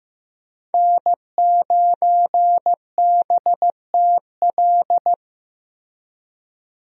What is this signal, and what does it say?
Morse code, heard as a keyed tone, about 15 wpm, 700 Hz. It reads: N9BTL
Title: USA Call Signs - 5 Characters - Repeated 3x - 15wpm